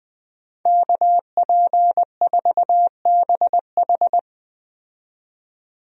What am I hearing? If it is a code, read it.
Morse code, heard as a keyed tone, about 20 wpm, 700 Hz. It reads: KP4BH
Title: DXCC-100 Call Signs - 5 Characters - 20wpm